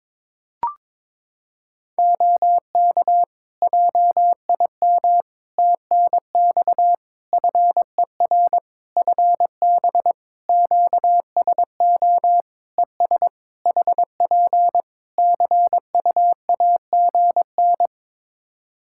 Morse code, heard as a keyed tone, about 21 wpm, 700 Hz. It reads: OK JIM TNX FER FB QSO ES HP CUAGN
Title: Practice QSO - Exchange 4 - Ending - 22wpm